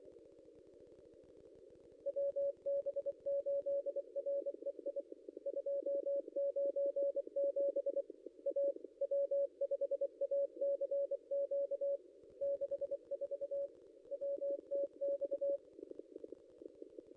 Morse code, heard as a keyed tone, about 25 wpm, 560 Hz. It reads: WB8REI 297 A W5ACQ 64 WTX
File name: contests / ss-cw-k5tr-op-n5rz -0800